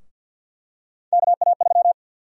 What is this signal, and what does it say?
Morse code, heard as a keyed tone, about 49 wpm, 700 Hz. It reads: KA3
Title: USA Call Sign Prefixes - 50wpm